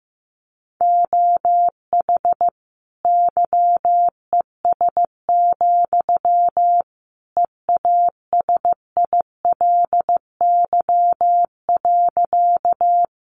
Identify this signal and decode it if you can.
Morse code, heard as a keyed tone, about 15 wpm, 700 Hz. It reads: OH YES, EASILY.
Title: Sentences from Top 900 Words 15wpm